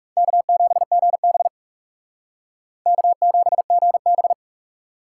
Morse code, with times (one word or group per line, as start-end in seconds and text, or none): 0.2-1.5 K7GB
2.9-4.3 K7GB